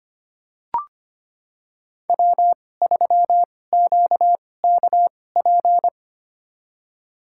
Morse code, heard as a keyed tone, about 25 wpm, spaced as for 18 wpm, 700 Hz. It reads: W3QKP
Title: USA Call Signs - 5 Characters - 2x Character Spacing - 25wpm